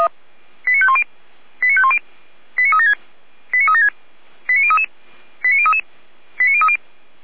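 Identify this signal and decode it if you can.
Five-tone selective call (ZVEI), heard as quick runs of five tones, tones of about 80 ms. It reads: 89510 89510 89376 89376 89033 89033 89030
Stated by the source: ZVEI five-tone call 89187 received on an SDR.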